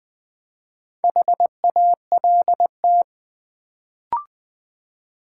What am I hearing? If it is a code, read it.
Morse code, heard as a keyed tone, about 20 wpm, 700 Hz. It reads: HALT
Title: Build A Word - 1 Letter At A Time - 3 to 5-Letter Words - 20wpm